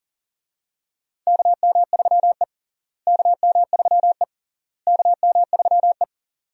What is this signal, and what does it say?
Morse code, heard as a keyed tone, about 40 wpm, 700 Hz. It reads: KM3E KM3E KM3E
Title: USA Call Signs - 4 Characters - Repeated 3x - 40wpm